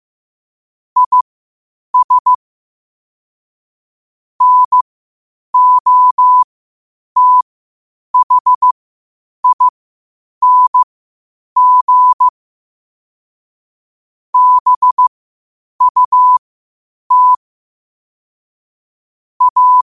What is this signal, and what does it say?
Morse code, heard as a keyed tone, about 15 wpm, spaced as for 8 wpm, 1000 Hz. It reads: IS NOTHING BUT A
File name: qotdfiles / 8wpm